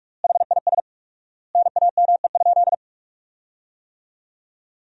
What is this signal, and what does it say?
Morse code, heard as a keyed tone, about 46 wpm, 700 Hz. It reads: HIS NAME?
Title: 4x Word-Spacing - Sentences from Top 400 Words - 45wpm